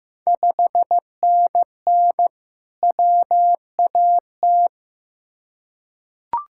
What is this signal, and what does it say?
Morse code, heard as a keyed tone, about 15 wpm, 700 Hz. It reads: 5NN WAT
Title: Ontario QSO Party Exchange 15wpm